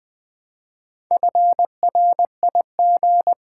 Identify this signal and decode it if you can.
Morse code, heard as a keyed tone, about 20 wpm, 690 Hz. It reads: FRIG
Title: Build A Word - 2 Letters At A Time - 8-Letter Words - 20wpm